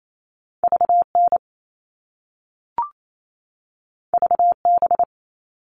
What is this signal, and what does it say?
Morse code, heard as a keyed tone, about 28 wpm, 700 Hz. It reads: VN VB
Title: All Cut Numbers - 2-Digit 28wpm